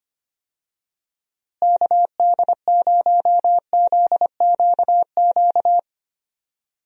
Morse code, heard as a keyed tone, about 25 wpm, 700 Hz. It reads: KD0ZQQ
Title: USA Call Signs - 6 Characters - Repeated 2x - 25wpm